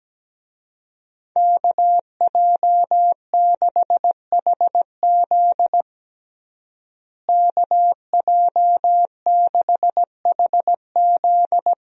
Morse code, heard as a keyed tone, about 17 wpm, 700 Hz. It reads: KJ6HZ KJ6HZ
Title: USA Call Signs - 5 Characters - Repeated 2x - 17wpm